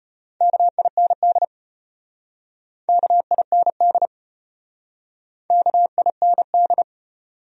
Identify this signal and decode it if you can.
Morse code, heard as a keyed tone, about 33 wpm, 700 Hz. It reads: KIND KIND KIND